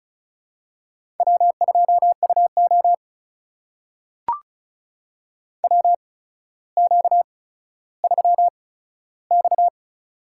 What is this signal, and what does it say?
Morse code, heard as a keyed tone, about 34 wpm, spaced as for 15 wpm, 700 Hz. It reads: W2UO WQ3X
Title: USA Call Signs - 4 Characters - 8x Character Spacing - 35wpm